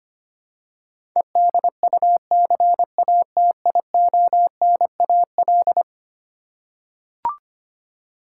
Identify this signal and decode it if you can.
Morse code, heard as a keyed tone, about 25 wpm, 700 Hz. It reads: EDUCATIONAL
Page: morse-code-ninja-practice-25wpm